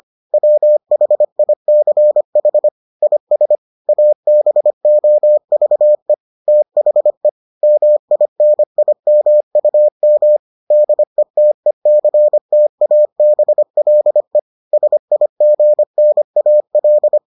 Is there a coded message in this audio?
Yes: Morse code from 0.3 to 17.2 s, heard as a keyed tone, about 25 wpm, 600 Hz. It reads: WHICH IS ABOVE THE MINIMUM DETECTABLE SIGNAL